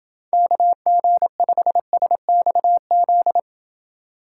Morse code, heard as a keyed tone, about 29 wpm, 700 Hz. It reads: KG5SXZ